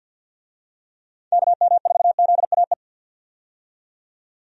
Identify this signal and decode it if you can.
Morse code, heard as a keyed tone, about 52 wpm, 700 Hz. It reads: KM4ZAE
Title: USA Call Signs - 6 Characters - 50wpm